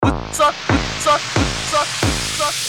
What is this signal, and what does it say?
There is no code.